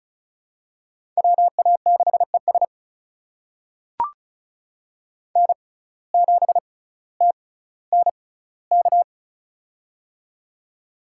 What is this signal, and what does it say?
Morse code, heard as a keyed tone, about 34 wpm, spaced as for 18 wpm, 700 Hz. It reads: WA6ES N7TNK
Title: USA Call Signs - 6x Character Spacing - 35wpm